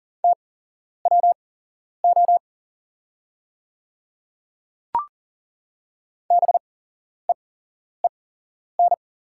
Morse code, heard as a keyed tone, about 39 wpm, spaced as for 11 wpm, 700 Hz. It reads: TWO BEEN
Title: 8x Character Spacing - Top 100 Words - 40wpm